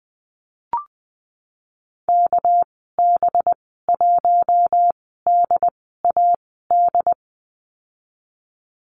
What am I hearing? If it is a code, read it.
Morse code, heard as a keyed tone, about 20 wpm, spaced as for 15 wpm, 700 Hz. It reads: KB1DAD
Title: 231 Lesson 37g - Letter x - Call Signs 20wpm